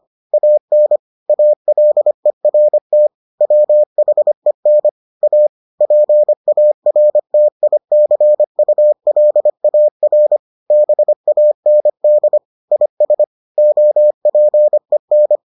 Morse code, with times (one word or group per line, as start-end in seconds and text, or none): 0.3-1.0 AN
1.3-3.1 ALERT
3.4-4.9 WHEN
5.2-5.5 A
5.8-10.4 PARTICULAR
10.7-12.4 BAND
12.7-13.2 IS
13.6-15.4 OPEN